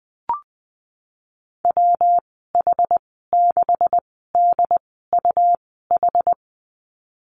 Morse code, heard as a keyed tone, about 20 wpm, spaced as for 15 wpm, 700 Hz. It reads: WH6DUH
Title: 179 Lesson 29g - Slash - Call Signs 20wpm